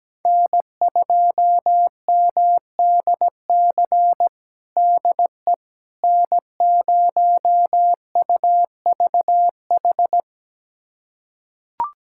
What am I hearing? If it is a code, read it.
Morse code, heard as a keyed tone, about 17 wpm, 700 Hz. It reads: N2MDC DE N0UVH
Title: USA Callsign DE Callsign - 5 Characters - 17wpm